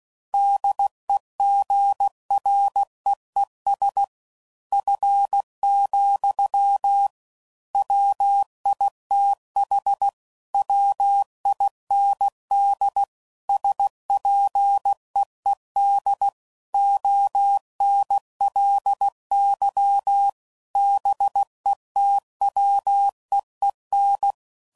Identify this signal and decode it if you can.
Morse code, heard as a keyed tone, about 16 wpm, 800 Hz. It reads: DEGREES F, WITH WIND SPEED ONLY BETWEEN
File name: CW Code Groups @15WPM/Weather Sets